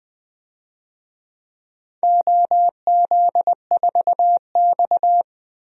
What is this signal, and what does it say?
Morse code, heard as a keyed tone, about 20 wpm, 700 Hz. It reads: OZ4X
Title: DXCC-100 Call Signs - 4 Characters - Repeated 3x - 20wpm